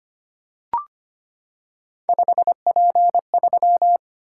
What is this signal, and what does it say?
Morse code, heard as a keyed tone, about 25 wpm, 700 Hz. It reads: 5P3
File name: Three Letter-Number Combinations - 25wpm